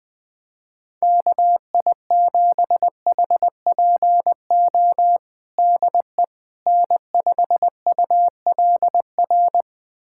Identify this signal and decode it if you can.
Morse code, heard as a keyed tone, about 20 wpm, 700 Hz. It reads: KI7HPO DE N5ULR
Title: USA Callsign DE Callsign 20wpm